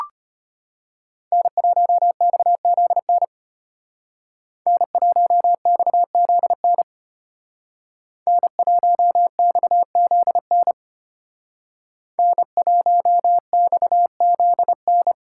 Morse code, heard as a keyed tone, about 32 wpm, 700 Hz. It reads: N1XZN N1XZN N1XZN N1XZN